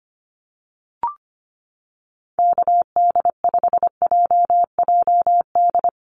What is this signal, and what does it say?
Morse code, heard as a keyed tone, about 25 wpm, 700 Hz. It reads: KD5JJD
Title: USA Call Signs Repeated 3x 25wpm